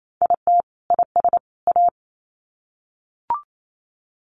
Morse code, heard as a keyed tone, about 28 wpm, 700 Hz. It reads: IT IS A
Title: Sets of 3 Words 28wpm